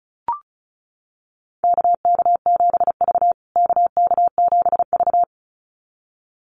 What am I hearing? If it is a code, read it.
Morse code, heard as a keyed tone, about 34 wpm, 700 Hz. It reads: KK7V KK7V
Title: USA Call Signs Repeated 2x 35wpm